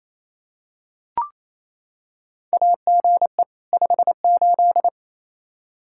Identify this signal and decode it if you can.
Morse code, heard as a keyed tone, about 28 wpm, 700 Hz. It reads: AGE 58
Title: Practice QSO - Exchange 3 - More Info - 28wpm